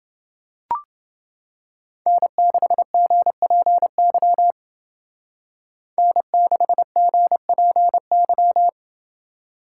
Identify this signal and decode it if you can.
Morse code, heard as a keyed tone, about 29 wpm, 700 Hz. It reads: N6GPY N6GPY